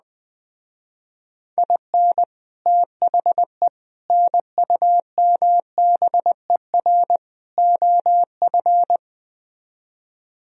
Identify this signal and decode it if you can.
Morse code, heard as a keyed tone, about 20 wpm, 700 Hz. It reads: IN THE NUMBER OF
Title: Speed-Racing - Sets of 4 Words 17wpm